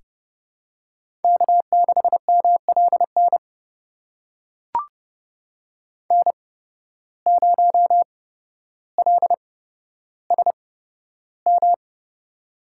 Morse code, heard as a keyed tone, about 29 wpm, spaced as for 13 wpm, 700 Hz. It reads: K6MLN N0LSM
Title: USA Call Signs - 5 Characters - 8x Character Spacing - 30wpm